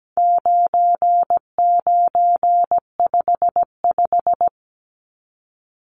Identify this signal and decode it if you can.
Morse code, heard as a keyed tone, about 17 wpm, 700 Hz. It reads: 9955
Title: Numbers - 4-Digit 17wpm